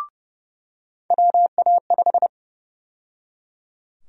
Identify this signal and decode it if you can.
Morse code, heard as a keyed tone, about 31 wpm, 700 Hz. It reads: WA5
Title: USA Call Sign Prefixes - 30wpm